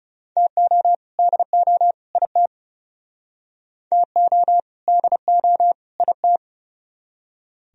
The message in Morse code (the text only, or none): TO DO IT TO DO IT